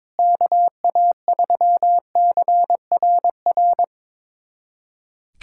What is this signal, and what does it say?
Morse code, heard as a keyed tone, about 22 wpm, 700 Hz. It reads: KA3CRR
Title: USA Callsign DE Callsign - 6 Characters - 22wpm